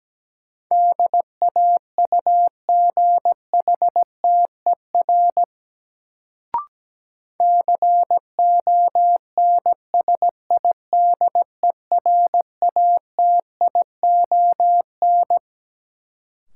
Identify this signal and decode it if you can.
Morse code, heard as a keyed tone, about 17 wpm, 700 Hz. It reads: DAUGHTER CONSIDERATION
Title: Sentences from Top 1500 Words - New Word Drill Alt - 17wpm